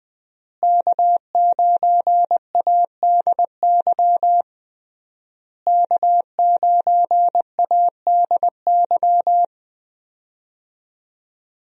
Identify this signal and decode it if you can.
Morse code, heard as a keyed tone, about 21 wpm, 700 Hz. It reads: K9ADY K9ADY
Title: USA Call Signs - 5 Characters - Repeated 3x - 20wpm